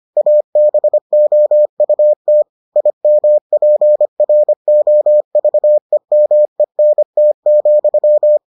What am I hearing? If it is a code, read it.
Morse code, heard as a keyed tone, about 25 wpm, 600 Hz. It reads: ABOUT IMPROVEMENT,